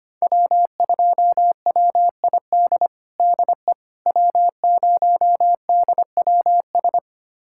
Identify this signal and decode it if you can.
Morse code, heard as a keyed tone, about 25 wpm, 700 Hz. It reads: W2WID DE W0DWS
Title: USA Callsign DE Callsign - 5 Characters - 25wpm